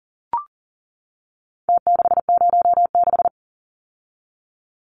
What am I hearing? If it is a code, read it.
Morse code, heard as a keyed tone, about 40 wpm, 700 Hz. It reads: T606
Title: Numbers - Contest Style - 4-Digit 40wpm